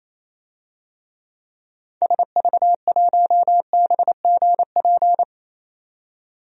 Morse code, heard as a keyed tone, about 28 wpm, 700 Hz. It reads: SV1BGP
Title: DXCC-100 Call Signs - 6 Characters - 28wpm